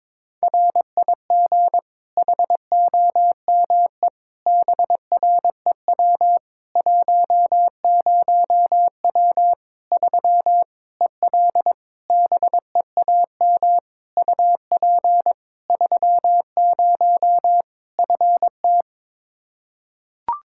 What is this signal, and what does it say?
Morse code, heard as a keyed tone, about 22 wpm, 700 Hz. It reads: RIG HOME BREW 10W 3 EL BEAM UP 30 FT
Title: Practice QSO - Exchange 2 - Info - 22wpm